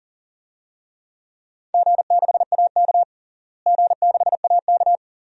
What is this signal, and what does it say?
Morse code, heard as a keyed tone, about 40 wpm, 700 Hz. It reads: G6AK G6AK
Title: DXCC-100 Call Signs - 4 Characters - Repeated 2x - 40wpm